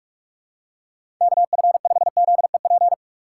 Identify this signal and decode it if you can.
Morse code, heard as a keyed tone, about 46 wpm, 700 Hz. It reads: KF5ZEP
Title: USA Call Signs - 6x Character Spacing - 45wpm